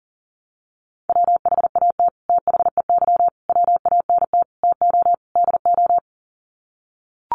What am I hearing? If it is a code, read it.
Morse code, heard as a keyed tone, about 37 wpm, 700 Hz. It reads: WHAT THEY WANT TO DO
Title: Sets of 5 Words Encore2 40wpm